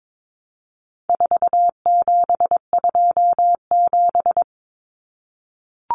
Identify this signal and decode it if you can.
Morse code, heard as a keyed tone, about 22 wpm, 700 Hz. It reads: 4727
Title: Numbers - 4-Digit 22wpm